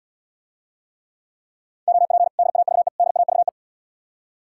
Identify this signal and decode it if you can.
Morse code, heard as a keyed tone, about 80 wpm, 700 Hz. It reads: OP DAVE DAVE